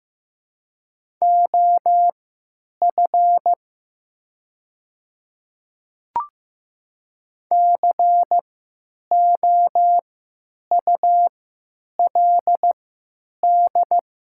Morse code, heard as a keyed tone, about 15 wpm, spaced as for 8 wpm, 690 Hz. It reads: OF COULD
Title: 3x Character Spacing - Top 100 Words - 15wpm